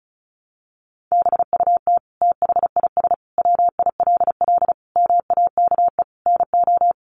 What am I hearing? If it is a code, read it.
Morse code, heard as a keyed tone, about 34 wpm, 700 Hz. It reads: BUT THIS WILL MAKE NO
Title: Sentences from Top 800 Words 35wpm